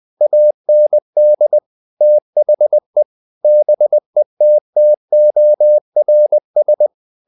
Morse code, heard as a keyed tone, about 20 wpm, 600 Hz. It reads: AND THE BETTORS
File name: the old man and the sea part 2 20wpm